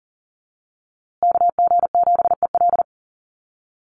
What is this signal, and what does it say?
Morse code, heard as a keyed tone, about 40 wpm, 700 Hz. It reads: KG7EL
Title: USA Call Signs Repeated 3x 40wpm